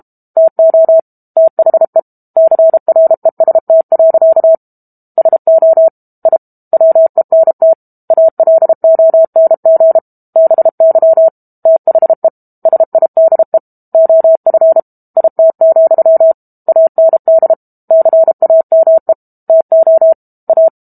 Morse code, heard as a keyed tone, about 32 wpm, 650 Hz. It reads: TO THE CREST. SO I WENT ALONG BY THE SIDE OF IT, AND CAME TO A
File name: Book Two / 06-Chapter 6 - 32 WPM